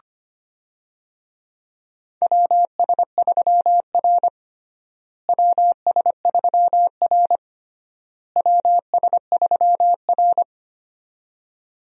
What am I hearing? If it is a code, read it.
Morse code, heard as a keyed tone, about 25 wpm, 700 Hz. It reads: WS3R WS3R WS3R
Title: USA Call Signs - 4 Characters - Repeated 3x - 25wpm